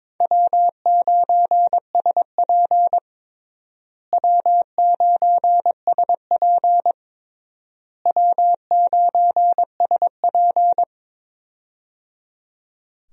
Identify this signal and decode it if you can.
Morse code, heard as a keyed tone, about 22 wpm, 700 Hz. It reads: W9SP W9SP W9SP
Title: USA Call Signs - 4 Characters - Repeated 3x - 22wpm